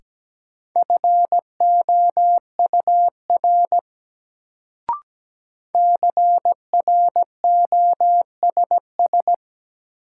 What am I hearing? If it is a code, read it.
Morse code, heard as a keyed tone, about 17 wpm, 700 Hz. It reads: FOUR CROSS
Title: Sentences from Top 300 Words - New Word Drill - 17wpm